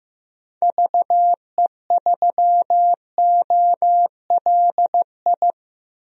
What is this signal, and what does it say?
Morse code, heard as a keyed tone, about 16 wpm, 700 Hz. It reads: VE3OLI
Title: DXCC-100 Call Signs - 6 Characters - Repeated 2x - 15wpm